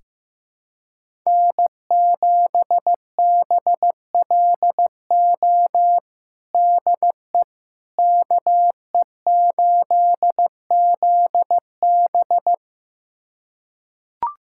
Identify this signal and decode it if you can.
Morse code, heard as a keyed tone, about 15 wpm, 700 Hz. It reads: N7BLO DE KE8ZB
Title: USA Callsign DE Callsign - 5 Characters - 15wpm